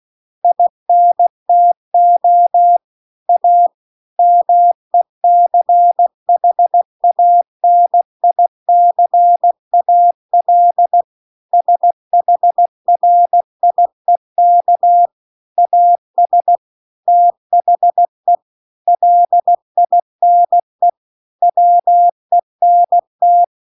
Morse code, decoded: INTO A MECHANICAL SHRIEK AS THE LINE WENT

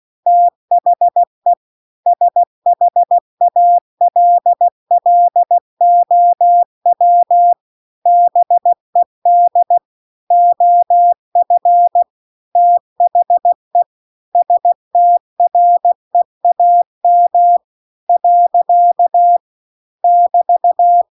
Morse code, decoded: THE SHALLOW BED OF THE STREAM . <BT>

